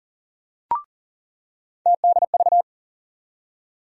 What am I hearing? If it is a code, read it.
Morse code, heard as a keyed tone, about 40 wpm, 700 Hz. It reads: TDV
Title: USA Call Sign Postfixes - 3 Letters 40wpm